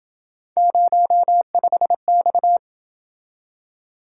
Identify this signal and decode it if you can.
Morse code, heard as a keyed tone, about 28 wpm, 700 Hz. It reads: 05X